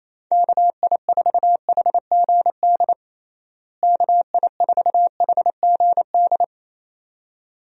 Morse code, heard as a keyed tone, about 28 wpm, 700 Hz. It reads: KI4HGD KI4HGD
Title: USA Call Signs - 6 Characters - Repeated 2x - 28wpm